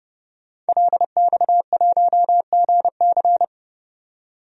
Morse code, heard as a keyed tone, about 29 wpm, 700 Hz. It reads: LX1GC
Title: DXCC-100 Call Signs - 5 Characters - 30wpm